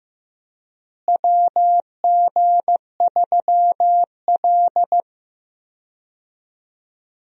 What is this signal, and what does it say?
Morse code, heard as a keyed tone, about 15 wpm, 700 Hz. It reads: WG3L
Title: USA Call Signs - 4 Characters - Repeated 3x - 15wpm